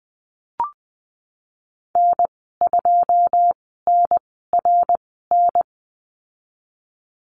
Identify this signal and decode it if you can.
Morse code, heard as a keyed tone, about 20 wpm, spaced as for 14 wpm, 700 Hz. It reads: N2NRN
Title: 082 Lesson 14g - Number 2 - Call Signs 20wpm